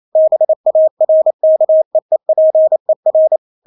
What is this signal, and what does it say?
Morse code, heard as a keyed tone, about 28 wpm, 630 Hz. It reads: BARKEEPER
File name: Witz9 28wpm